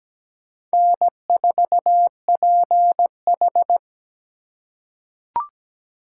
Morse code, heard as a keyed tone, about 17 wpm, 700 Hz. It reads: N4PH
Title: USA Call Signs - 4 Characters - 4x Character Spacing - 17wpm